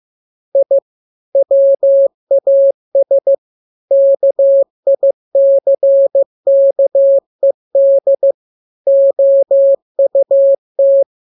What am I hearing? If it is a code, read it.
Morse code, heard as a keyed tone, about 15 wpm, 550 Hz. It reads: I WAS KICKED OUT
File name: jokes / joke 3 15 0000